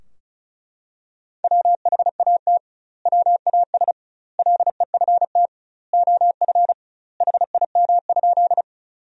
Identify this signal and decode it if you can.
Morse code, heard as a keyed tone, about 36 wpm, 700 Hz. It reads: WHAT WAS LEFT OF HIM?
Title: 2x Word-Spacing - Sentences from Top 300 Words - 35wpm